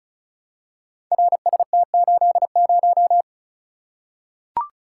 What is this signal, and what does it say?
Morse code, heard as a keyed tone, about 36 wpm, 700 Hz. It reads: RST80
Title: Code Groups - 5 Letter:Numbers - 35wpm